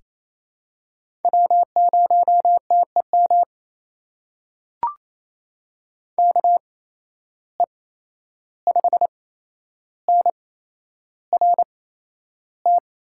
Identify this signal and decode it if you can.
Morse code, heard as a keyed tone, about 28 wpm, spaced as for 11 wpm, 700 Hz. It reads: W0TEM KE5NRT
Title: USA Call Signs - 8x Character Spacing - 28wpm